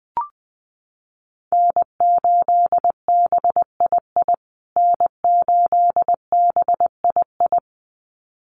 Morse code, heard as a keyed tone, about 20 wpm, 700 Hz. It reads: N8BII N8BII
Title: USA Call Signs Repeated 2x 20wpm